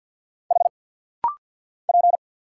Morse code, heard as a keyed tone, about 50 wpm, 700 Hz. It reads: H P
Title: Single Letter-Number Alt - Rapid-Fire 50wpm